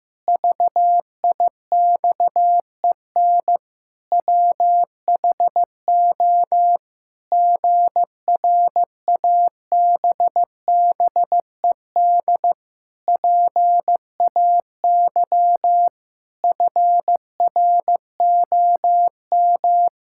Morse code, heard as a keyed tone, about 15 wpm, 700 Hz. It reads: VIXEN WHO GRABBED PAY FROM
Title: Pangram Challenge v2 - 15wpm